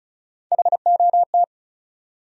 Morse code, heard as a keyed tone, about 36 wpm, 700 Hz. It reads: HOT
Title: Top 100 Words 35wpm